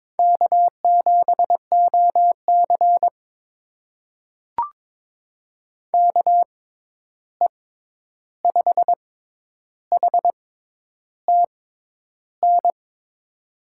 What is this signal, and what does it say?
Morse code, heard as a keyed tone, about 22 wpm, spaced as for 10 wpm, 700 Hz. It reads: K7OC KE5HTN
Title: USA Call Signs - 6x Character Spacing - 22wpm